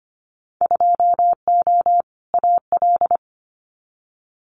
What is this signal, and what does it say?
Morse code, heard as a keyed tone, about 25 wpm, 700 Hz. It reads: 2O AL
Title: Winter Field Day Exchanges Encore 25wpm